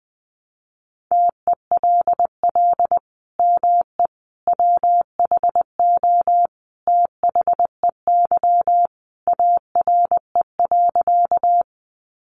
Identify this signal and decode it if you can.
Morse code, heard as a keyed tone, about 20 wpm, 700 Hz. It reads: TELL ME WHO THEY ARE.
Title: Sentences from Top 700 Words 20wpm